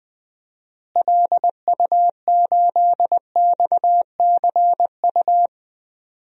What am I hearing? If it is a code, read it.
Morse code, heard as a keyed tone, about 20 wpm, 700 Hz. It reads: LU8XCU
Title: DXCC-100 Call Signs - 6 Characters - Repeated 2x - 20wpm